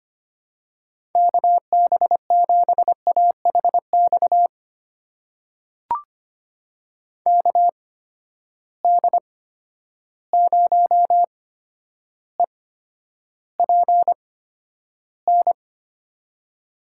Morse code, heard as a keyed tone, about 25 wpm, spaced as for 11 wpm, 700 Hz. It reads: KB7AHX KD0EPN
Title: USA Call Signs - 6 Characters - 8x Character Spacing - 25wpm